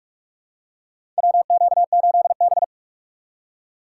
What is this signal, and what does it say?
Morse code, heard as a keyed tone, about 46 wpm, 700 Hz. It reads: WQ8B
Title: USA Call Signs - 4x Character Spacing - 45wpm